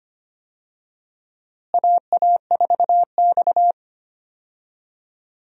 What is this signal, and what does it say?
Morse code, heard as a keyed tone, about 25 wpm, 700 Hz. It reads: AA4X
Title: USA Call Signs - 4 Characters - Repeated 3x - 25wpm